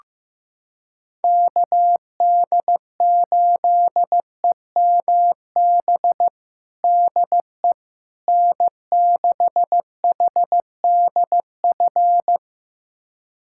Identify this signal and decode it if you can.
Morse code, heard as a keyed tone, about 16 wpm, 700 Hz. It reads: KD8EMB DE N6HDF
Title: USA Callsign DE Callsign 15wpm